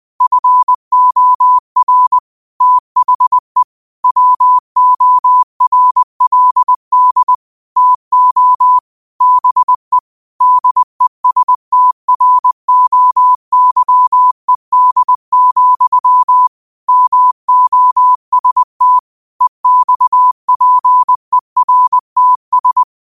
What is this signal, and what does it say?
Morse code, heard as a keyed tone, about 20 wpm, 1000 Hz. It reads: FOR THE WORLD TO BE DESTROYED, MOST EXPERTS
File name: 20wpm